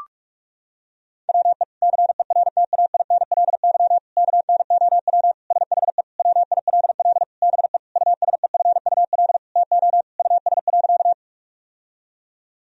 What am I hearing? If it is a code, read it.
Morse code, heard as a keyed tone, about 42 wpm, 700 Hz. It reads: WE CERTAINLY KNOW SHE WILL BE USEFUL TO US.
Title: Sentences from Top 1000 Words - Review Part 4 - 45wpm